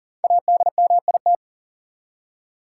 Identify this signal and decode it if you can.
Morse code, heard as a keyed tone, about 40 wpm, 700 Hz. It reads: ADMIT
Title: Five Letter Words - 40wpm